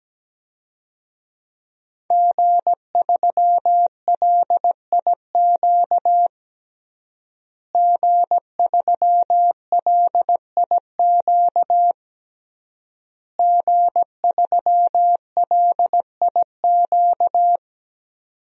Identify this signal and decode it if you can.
Morse code, heard as a keyed tone, about 17 wpm, 700 Hz. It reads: G3LIQ G3LIQ G3LIQ
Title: DXCC-100 Call Signs - 5 Characters - Repeated 3x - 17wpm